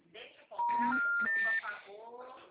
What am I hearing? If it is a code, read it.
Special information tone, with speech.